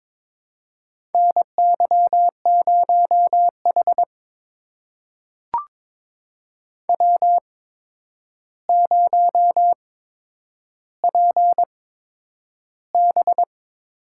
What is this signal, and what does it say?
Morse code, heard as a keyed tone, about 22 wpm, spaced as for 9 wpm, 700 Hz. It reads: NY0H W0PB
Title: USA Call Signs - 4 Characters - 8x Character Spacing - 22wpm